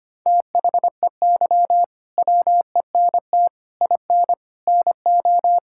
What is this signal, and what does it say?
Morse code, heard as a keyed tone, about 25 wpm, 700 Hz. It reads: THEY WENT IN NO
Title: Sentences from Top 1000 Words - Review Part 5 - 25wpm